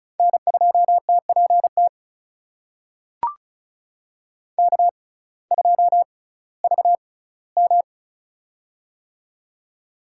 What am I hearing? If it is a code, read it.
Morse code, heard as a keyed tone, about 34 wpm, spaced as for 19 wpm, 700 Hz. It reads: N2TPT K2VM
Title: USA Call Signs - 6x Character Spacing - 35wpm